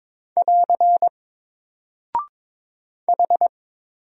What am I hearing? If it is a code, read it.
Morse code, heard as a keyed tone, about 22 wpm, 700 Hz. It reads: <AR> H
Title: Single Character (Letter:Number:Punctuation:Prosign) 22wpm